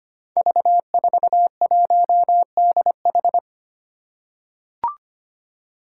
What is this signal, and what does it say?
Morse code, heard as a keyed tone, about 25 wpm, 700 Hz. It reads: V41DH
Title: Five Challenging Letter-Number Combinations 25wpm